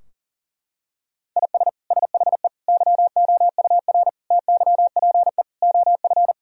Morse code, heard as a keyed tone, about 40 wpm, 700 Hz. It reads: IS SHE YOUR TYPE OF
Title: Sentences from Top 1000 Words - Review Part 3 - 40wpm